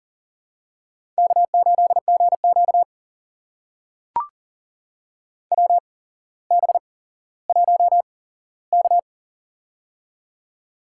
Morse code, heard as a keyed tone, about 40 wpm, spaced as for 17 wpm, 700 Hz. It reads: K8GQ WB1K
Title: USA Call Signs - 4 Characters - 8x Character Spacing - 40wpm